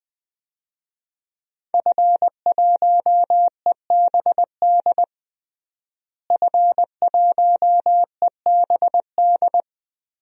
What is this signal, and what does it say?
Morse code, heard as a keyed tone, about 20 wpm, 700 Hz. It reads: F1EBD F1EBD
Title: DXCC-100 Call Signs - 5 Characters - Repeated 3x - 20wpm